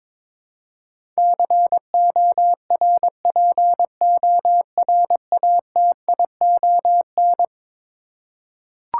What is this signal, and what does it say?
Morse code, heard as a keyed tone, about 22 wpm, 700 Hz. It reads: CORPORATION
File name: practice / 22wpm / Eleven Letter Words - 22wpm